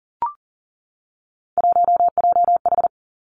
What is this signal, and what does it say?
Morse code, heard as a keyed tone, about 38 wpm, 700 Hz. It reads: 1JH
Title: Three Challenging Letter-Number Combinations 40wpm